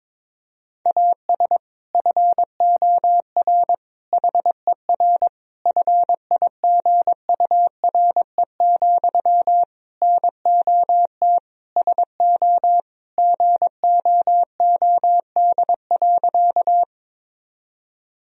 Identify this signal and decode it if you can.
Morse code, heard as a keyed tone, about 22 wpm, 700 Hz. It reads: AS FOR HER FIGURE, NOT SO GOOD.
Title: Sentences from Top 1000 Words - Review Part 2 - 22wpm